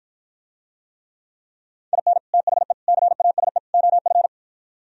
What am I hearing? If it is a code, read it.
Morse code, heard as a keyed tone, about 52 wpm, 700 Hz. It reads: IN THE CASE OF